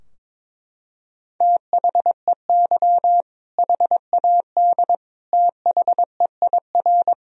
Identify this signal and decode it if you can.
Morse code, heard as a keyed tone, about 21 wpm, 700 Hz. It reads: THEY HAD THEIR
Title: Sentences from Top 1000 Words - Review Part 3 - 22wpm